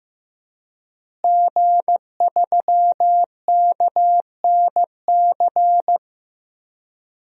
Morse code, heard as a keyed tone, about 15 wpm, 700 Hz. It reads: G3KNC
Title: DXCC-100 Call Signs - 5 Characters - Repeated 3x - 15wpm